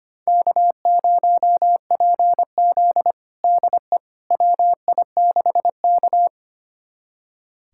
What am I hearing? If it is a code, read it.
Morse code, heard as a keyed tone, about 25 wpm, 700 Hz. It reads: K0PZ DE WI6K
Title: USA Callsign DE Callsign - 4 Characters - 25wpm